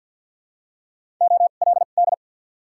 Morse code, heard as a keyed tone, about 52 wpm, spaced as for 37 wpm, 700 Hz. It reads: OLD
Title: 2x Character Spacing - Top 300 Words - 50wpm